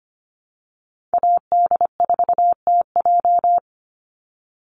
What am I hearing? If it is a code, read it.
Morse code, heard as a keyed tone, about 25 wpm, 700 Hz. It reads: AD4TJ
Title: Call Signs - 25wpm